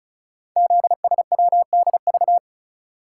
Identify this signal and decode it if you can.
Morse code, heard as a keyed tone, about 36 wpm, 690 Hz. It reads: ZSWDV
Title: Code Groups - 5 Letters - 35wpm